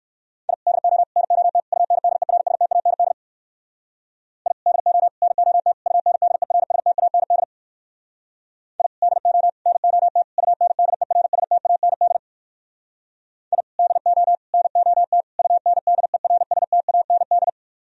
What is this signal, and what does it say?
Morse code, heard as a keyed tone, about 58 wpm, 700 Hz. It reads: I DO NOT UNDERSTAND I DO NOT UNDERSTAND I DO NOT UNDERSTAND I DO NOT UNDERSTAND